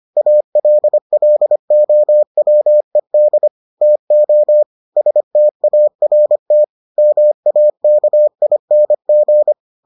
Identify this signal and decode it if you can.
Morse code, heard as a keyed tone, about 25 wpm, 600 Hz. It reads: ALLOWED TO START MAKING